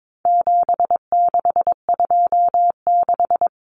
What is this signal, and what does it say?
Morse code, heard as a keyed tone, about 22 wpm, 700 Hz. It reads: 7626
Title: Numbers - 4-Digit 22wpm